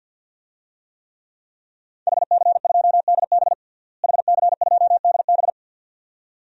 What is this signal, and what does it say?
Morse code, heard as a keyed tone, about 50 wpm, 700 Hz. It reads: HC2DB HC2DB
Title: DXCC-100 Call Signs - 5 Characters - Repeated 2x - 50wpm